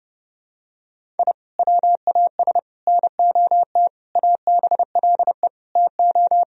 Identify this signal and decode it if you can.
Morse code, heard as a keyed tone, about 30 wpm, 700 Hz. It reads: I WAS NOT ABLE TO
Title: Sentences from Top 1000 Words - Review Part 2 - 30wpm